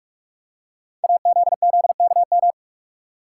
Morse code, heard as a keyed tone, about 44 wpm, 700 Hz. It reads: AZZKM
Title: Code Groups - 5 Letters - 45wpm